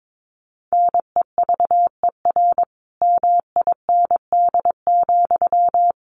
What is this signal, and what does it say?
Morse code, heard as a keyed tone, about 22 wpm, 700 Hz. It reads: NEVER MIND,
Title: Sentences from Top 700 Words 22wpm